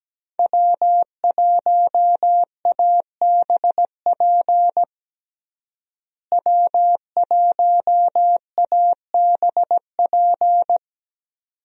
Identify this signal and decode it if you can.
Morse code, heard as a keyed tone, about 17 wpm, 700 Hz. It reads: W1ABP W1ABP
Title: USA Call Signs - 5 Characters - Repeated 2x - 17wpm